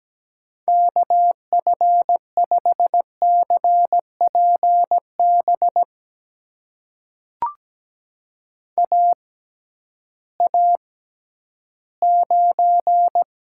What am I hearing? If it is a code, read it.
Morse code, heard as a keyed tone, about 17 wpm, spaced as for 11 wpm, 700 Hz. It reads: KF5CPB AA9
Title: USA Call Signs - 6x Character Spacing - 17wpm